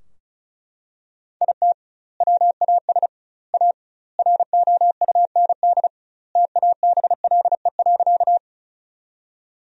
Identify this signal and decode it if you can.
Morse code, heard as a keyed tone, about 36 wpm, 700 Hz. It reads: IT WAS A ROUND TABLE.
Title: 2x Word-Spacing - Sentences from Top 400 Words - 35wpm